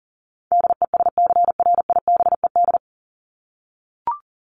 Morse code, heard as a keyed tone, about 40 wpm, 700 Hz. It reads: DESCRIBED
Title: ED Suffix - Top 100 Words with Base Words 40wpm